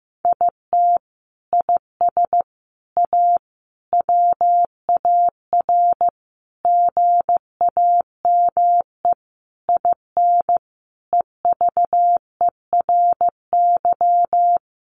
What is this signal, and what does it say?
Morse code, heard as a keyed tone, about 15 wpm, 700 Hz. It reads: IT IS A WAR GAME IN EVERY
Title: Sentences from Top 1500 Words 15wpm